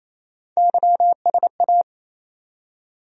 Morse code, heard as a keyed tone, about 27 wpm, 700 Hz. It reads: YSA
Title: USA Call Sign Postfixes - 3 Letters 28wpm